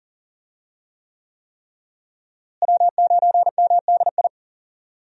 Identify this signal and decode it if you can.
Morse code, heard as a keyed tone, about 40 wpm, 700 Hz. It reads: W9MDI
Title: USA Call Signs - 5 Characters - Repeated 3x - 40wpm